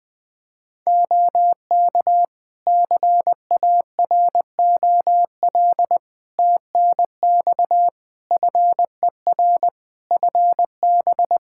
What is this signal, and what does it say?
Morse code, heard as a keyed tone, about 21 wpm, 700 Hz. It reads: OK CAROL TNX FER FB
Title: Practice QSO - Exchange 4 - Ending - 20wpm